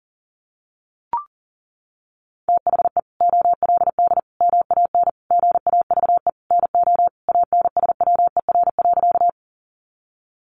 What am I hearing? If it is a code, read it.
Morse code, heard as a keyed tone, about 40 wpm, 700 Hz. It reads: THE OLD MAN GAVE NO ANSWER.
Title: Sentences from Top 500 Words 40wpm